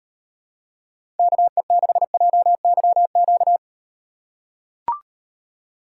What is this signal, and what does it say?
Morse code, heard as a keyed tone, about 38 wpm, 700 Hz. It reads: KE6JYQ